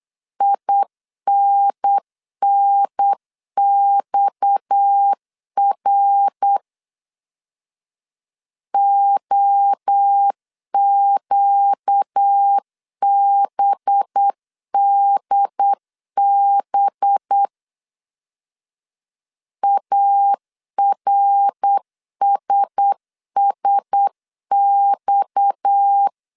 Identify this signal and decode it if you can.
Morse code, heard as a keyed tone, about 8 wpm, 800 Hz. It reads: INNXR OQBDB ARSSX
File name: CW / DET / 1 6 1500C S B 8wpm 5142